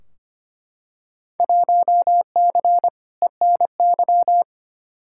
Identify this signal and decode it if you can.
Morse code, heard as a keyed tone, about 25 wpm, 700 Hz. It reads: 1C ENY
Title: Field Day Exchanges Encore - 25wpm